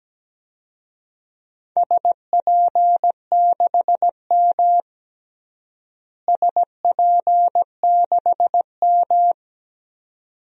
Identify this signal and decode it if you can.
Morse code, heard as a keyed tone, about 17 wpm, 700 Hz. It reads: SP6M SP6M
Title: DXCC-100 Call Signs - 4 Characters - Repeated 2x - 17wpm